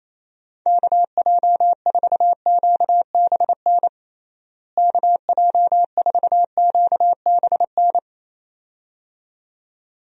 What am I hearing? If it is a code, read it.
Morse code, heard as a keyed tone, about 28 wpm, 700 Hz. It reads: KJ4QBN KJ4QBN
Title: USA Call Signs - 6 Characters - Repeated 2x - 28wpm